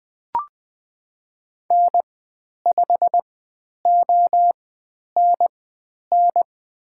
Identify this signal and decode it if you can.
Morse code, heard as a keyed tone, about 20 wpm, spaced as for 10 wpm, 700 Hz. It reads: N5ONN
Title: 095 Lesson 16g - Letter c - Call Signs 20wpm